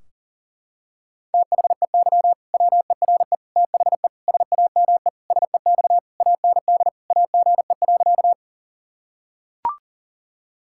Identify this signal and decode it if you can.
Morse code, heard as a keyed tone, about 40 wpm, 700 Hz. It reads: THEY WERE THE SAME SEX AND AGE.
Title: Sentences from Top 1600 Words 40wpm